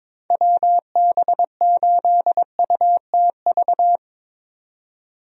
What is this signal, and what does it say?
Morse code, heard as a keyed tone, about 22 wpm, 700 Hz. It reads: WB8UTV
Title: USA Call Signs - 6 Characters - 22wpm